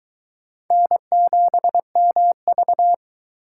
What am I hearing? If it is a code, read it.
Morse code, heard as a keyed tone, about 24 wpm, 700 Hz. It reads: N7MV